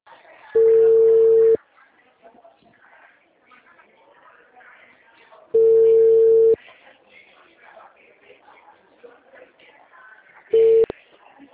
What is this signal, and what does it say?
Ringback tone (China, 450 Hz), with speech.